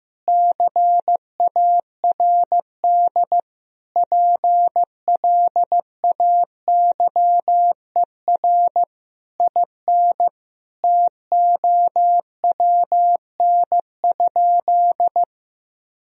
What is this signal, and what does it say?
Morse code, heard as a keyed tone, about 15 wpm, 700 Hz. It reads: CARD PLAYER IN TOWN?
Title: Sentences from Top 1000 Words - Review Part 4 - 15wpm